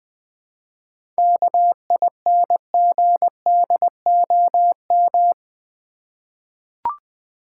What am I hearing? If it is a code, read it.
Morse code, heard as a keyed tone, about 21 wpm, 700 Hz. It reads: KINGDOM
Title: 7 Letter Words - First 3 Letters Given - 20wpm